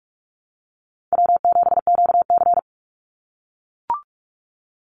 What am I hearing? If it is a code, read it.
Morse code, heard as a keyed tone, about 44 wpm, 700 Hz. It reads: W7QC
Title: Easy Call Signs - 45wpm